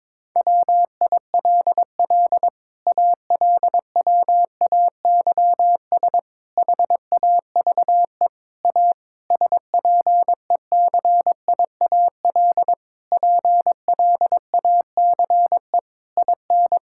Morse code, heard as a keyed tone, about 22 wpm, 700 Hz. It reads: WILL ALWAYS HAVE A SPECIAL PLACE IN
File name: Sets of 9 Words - 22wpm